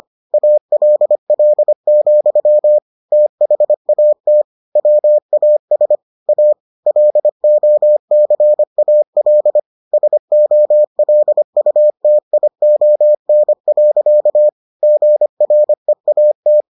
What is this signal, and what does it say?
Morse code, heard as a keyed tone, about 25 wpm, 600 Hz. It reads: ALL, THAT WAS A LOCAL SOLUTION. GREAT